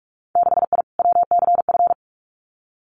Morse code, heard as a keyed tone, about 46 wpm, 700 Hz. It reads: 6I WCF
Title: Winter Field Day Exchanges Encore 45wpm